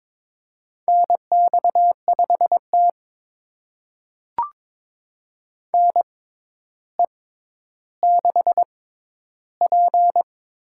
Morse code, heard as a keyed tone, about 22 wpm, spaced as for 11 wpm, 700 Hz. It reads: NX5T NE6P
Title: USA Call Signs - 4 Characters - 6x Character Spacing - 22wpm